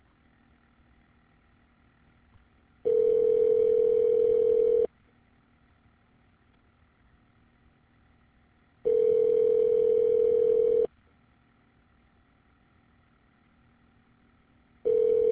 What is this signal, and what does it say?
Ringback tone (North America), 440 + 480 Hz.